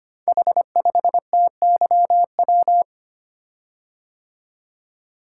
Morse code, heard as a keyed tone, about 25 wpm, 700 Hz. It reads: H5TYW
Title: Code Groups - 5 Letter:Numbers - 25wpm